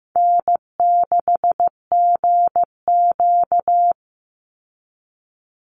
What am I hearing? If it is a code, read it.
Morse code, heard as a keyed tone, about 15 wpm, 700 Hz. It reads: N6GQ
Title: Easy Call Signs Repeated 2x - 15wpm